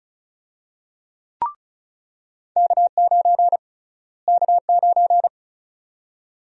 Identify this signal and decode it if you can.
Morse code, heard as a keyed tone, about 36 wpm, 700 Hz. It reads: K9 K9
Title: USA Call Sign Prefixes - Repeated 2x - 35wpm